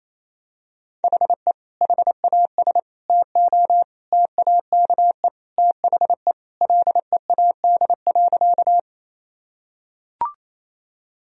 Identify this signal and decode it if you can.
Morse code, heard as a keyed tone, about 28 wpm, 700 Hz. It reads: HE HAS TO TAKE THE LEAD.
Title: Sentences from Top 1000 Words - Review Part 2 - 28wpm